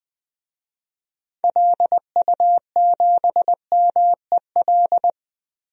Morse code, heard as a keyed tone, about 20 wpm, 700 Hz. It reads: LU7MEL
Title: DXCC-100 Call Signs - 6 Characters - Repeated 3x - 20wpm